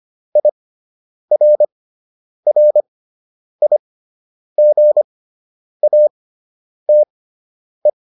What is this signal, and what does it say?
Morse code, heard as a keyed tone, about 25 wpm, spaced as for 9 wpm, 600 Hz. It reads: IRRIGATE